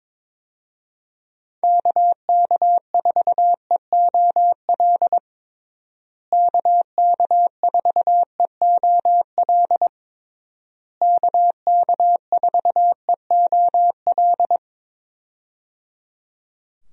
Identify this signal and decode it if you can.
Morse code, heard as a keyed tone, about 22 wpm, 700 Hz. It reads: KK4EOL KK4EOL KK4EOL
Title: USA Call Signs - 6 Characters - Repeated 3x - 22wpm